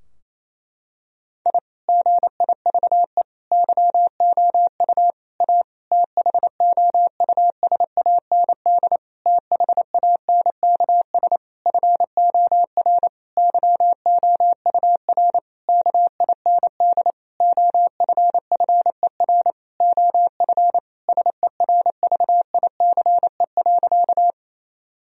Morse code, heard as a keyed tone, about 28 wpm, 700 Hz. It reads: I GIVE YOU A THOUSAND THANKS FOR YOUR KIND OFFER OF SERVICE.
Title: Sentences from Top 1000 Words - Review Part 4 - 28wpm